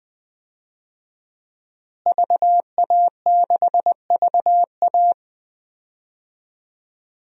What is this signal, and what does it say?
Morse code, heard as a keyed tone, about 20 wpm, 700 Hz. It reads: VA6VA
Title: DXCC-100 Call Signs - 5 Characters - Repeated 3x - 20wpm